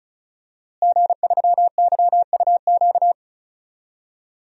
Morse code, heard as a keyed tone, about 36 wpm, 700 Hz. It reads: G3YUQ
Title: DXCC-100 Call Signs - 5 Characters - Repeated 3x - 35wpm